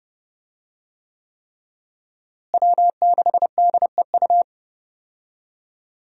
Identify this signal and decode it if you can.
Morse code, heard as a keyed tone, about 30 wpm, 700 Hz. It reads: W6DEU
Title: USA Call Signs - 5 Characters - Repeated 3x - 30wpm